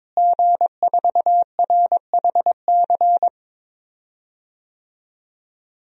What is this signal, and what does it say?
Morse code, heard as a keyed tone, about 22 wpm, 700 Hz. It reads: G4RHC
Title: DXCC-100 Call Signs - 5 Characters - Repeated 3x - 22wpm